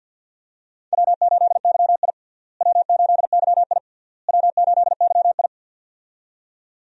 Morse code, heard as a keyed tone, about 53 wpm, 700 Hz. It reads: W8YI W8YI W8YI
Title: USA Call Signs - 4 Characters - Repeated 3x - 50wpm